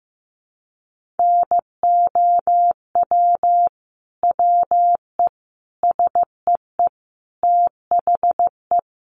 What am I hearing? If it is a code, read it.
Morse code, heard as a keyed tone, about 15 wpm, 700 Hz. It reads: NOW WE SEE THE
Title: Sentences from Top 200 Words 15wpm